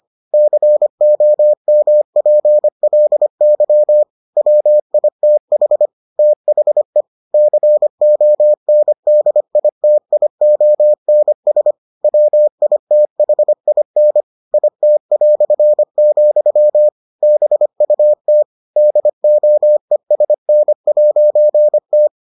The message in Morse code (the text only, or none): COMPLY WITH THE CONDITIONS WITHIN IT", BUT DOESN'T